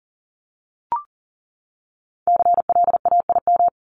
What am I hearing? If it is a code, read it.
Morse code, heard as a keyed tone, about 40 wpm, 700 Hz. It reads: CLAIM
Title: Top 1251-1500 Words 40wpm